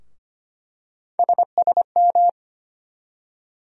Morse code, heard as a keyed tone, about 25 wpm, 700 Hz. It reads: SSM
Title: USA Call Sign Postfixes - 3 Letters 25wpm